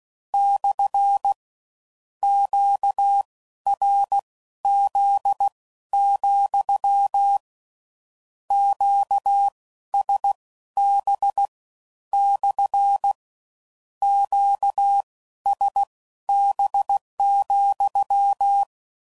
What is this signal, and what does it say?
Morse code, heard as a keyed tone, about 16 wpm, spaced as for 12 wpm, 800 Hz. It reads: / QRZ, QSB / QSB,
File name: CW Code Groups @15WPM / Q Calls / Q Calls SPSL